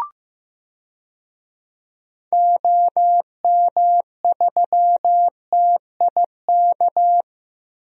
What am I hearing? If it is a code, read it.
Morse code, heard as a keyed tone, about 15 wpm, 700 Hz. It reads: OM3TIK
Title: DXCC-100 Call Signs - 6 Characters - Repeated 3x - 15wpm